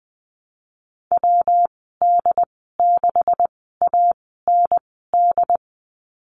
Morse code, heard as a keyed tone, about 20 wpm, spaced as for 15 wpm, 700 Hz. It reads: WD6AND
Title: 140 Lesson 23g - Letter f - Call Signs 20wpm